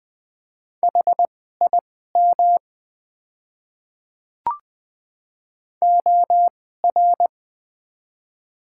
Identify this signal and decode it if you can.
Morse code, heard as a keyed tone, about 20 wpm, spaced as for 15 wpm, 700 Hz. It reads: HIM OR
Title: 2x Character Spacing - Top 100 Words - 20wpm